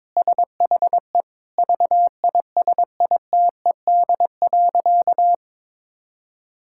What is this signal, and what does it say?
Morse code, heard as a keyed tone, about 22 wpm, 700 Hz. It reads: SHE VISITED.
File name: practice / 22wpm / Sentences with 2-Word Phrases - 22wpm